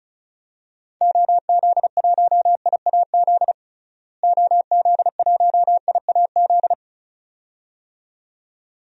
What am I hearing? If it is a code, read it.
Morse code, heard as a keyed tone, about 34 wpm, 700 Hz. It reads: OZ1IAZ OZ1IAZ
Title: DXCC-100 Call Signs - 6 Characters - Repeated 2x - 35wpm